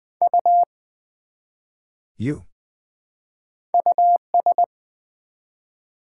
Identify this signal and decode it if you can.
Morse code, heard as a keyed tone, about 20 wpm, 700 Hz. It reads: U US
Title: Build A Word - 1 Letter At A Time - 3-Letter Words - 20wpm